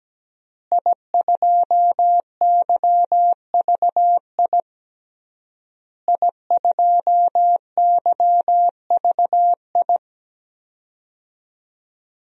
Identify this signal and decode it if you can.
Morse code, heard as a keyed tone, about 17 wpm, 700 Hz. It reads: I2YVI I2YVI
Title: DXCC-100 Call Signs - 5 Characters - Repeated 3x - 17wpm